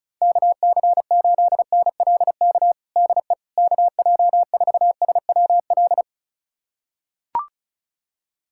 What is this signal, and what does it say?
Morse code, heard as a keyed tone, about 34 wpm, 700 Hz. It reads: KC8NLK DE KJ4SWL
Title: USA Callsign DE Callsign 35wpm